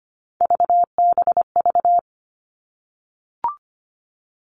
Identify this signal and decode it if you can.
Morse code, heard as a keyed tone, about 25 wpm, 700 Hz. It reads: VBV
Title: All Cut Numbers - 3-Digit 25wpm